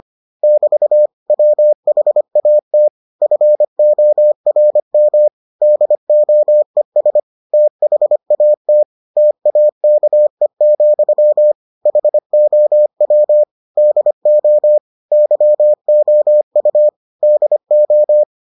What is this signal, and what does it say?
Morse code, heard as a keyed tone, about 25 wpm, 600 Hz. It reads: <BT> WHAT FORM DOES THAT TAKE, HOW DO YOU DO